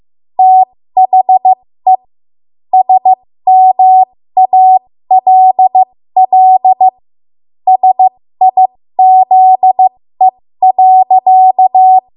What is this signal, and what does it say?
Morse code, heard as a keyed tone, about 15 wpm, spaced as for 13 wpm, 750 Hz. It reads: THE SMALL SIZE.
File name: Morse / 13WPM